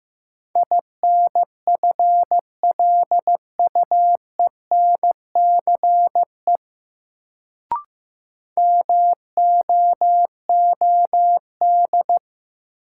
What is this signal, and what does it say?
Morse code, heard as a keyed tone, about 15 wpm, 700 Hz. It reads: INFLUENCE MOOD
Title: Sentences from Top 1100 Words - New Word Drill Alt - 15wpm